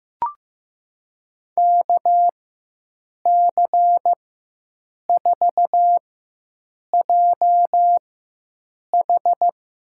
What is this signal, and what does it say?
Morse code, heard as a keyed tone, about 15 wpm, spaced as for 7 wpm, 700 Hz. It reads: KC4JH
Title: USA Call Signs - 4x Character Spacing - 15wpm